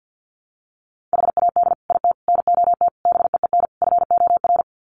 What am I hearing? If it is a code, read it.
Morse code, heard as a keyed tone, about 50 wpm, 700 Hz. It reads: HAD IT NOT BEEN FOR
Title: Sets of 5 Words Encore2 50wpm